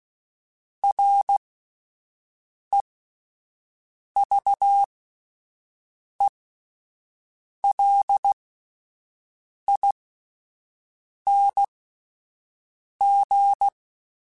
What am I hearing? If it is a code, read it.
Morse code, heard as a keyed tone, about 16 wpm, spaced as for 5 wpm, 800 Hz. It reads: REVELING